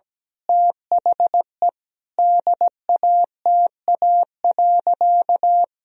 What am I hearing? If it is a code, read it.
Morse code, heard as a keyed tone, about 17 wpm, 700 Hz. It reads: THE DATA.
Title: Sentences from Top 1000 Words - Review Part 3 - 17wpm